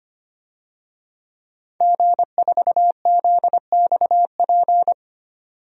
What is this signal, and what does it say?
Morse code, heard as a keyed tone, about 25 wpm, 700 Hz. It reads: G4ZXP
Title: DXCC-100 Call Signs - 5 Characters - 25wpm